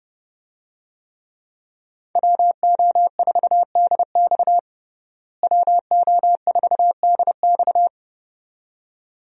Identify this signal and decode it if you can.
Morse code, heard as a keyed tone, about 29 wpm, 700 Hz. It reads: WO4DX WO4DX
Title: USA Call Signs - 5 Characters - Repeated 2x - 30wpm